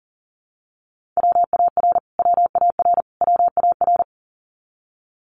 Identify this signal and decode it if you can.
Morse code, heard as a keyed tone, about 40 wpm, 700 Hz. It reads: WAR WAR WAR
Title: Top 500 Words Repeated 40wpm